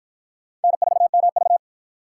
Morse code, heard as a keyed tone, about 52 wpm, 700 Hz. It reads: N4MV